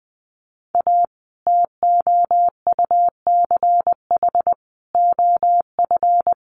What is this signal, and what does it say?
Morse code, heard as a keyed tone, about 20 wpm, 700 Hz. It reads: A TOUCH OF
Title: A Sound Of Thunder 20wpm